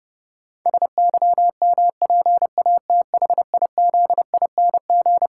Morse code, heard as a keyed tone, about 30 wpm, 700 Hz. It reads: SYMPATHIZING
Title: Pangram Challenge v2 - 30wpm